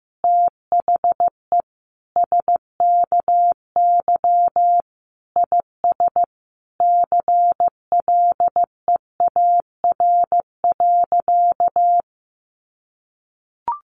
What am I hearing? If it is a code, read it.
Morse code, heard as a keyed tone, about 15 wpm, 700 Hz. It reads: THE SKY IS CLEAR.
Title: Sentences from Top 1200 Words 15wpm